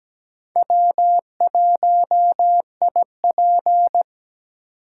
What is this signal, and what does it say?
Morse code, heard as a keyed tone, about 17 wpm, 700 Hz. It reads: W1IP
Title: USA Call Signs - 4 Characters - 3x Character Spacing - 17wpm